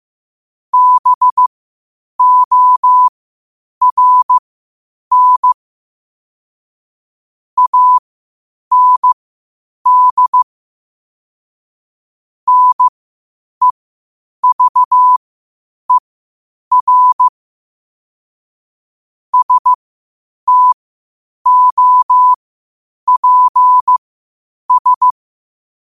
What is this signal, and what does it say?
Morse code, heard as a keyed tone, about 15 wpm, spaced as for 8 wpm, 1000 Hz. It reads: BORN AND NEVER STOPS
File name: qotdfiles / 8wpm